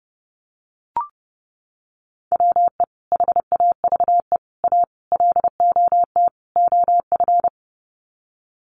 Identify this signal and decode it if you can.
Morse code, heard as a keyed tone, about 30 wpm, 700 Hz. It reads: WE HAVE A LOT OF
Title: Sets of 5 Words 30wpm